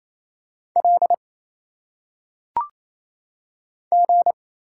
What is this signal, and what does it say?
Morse code, heard as a keyed tone, about 27 wpm, 700 Hz. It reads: L G
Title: Single Letters - Phonetic Pronunciation 28wpm